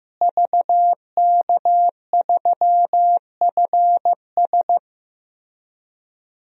Morse code, heard as a keyed tone, about 15 wpm, 700 Hz. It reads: VK3FS
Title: USA Call Signs - 5 Characters - Repeated 2x - 15wpm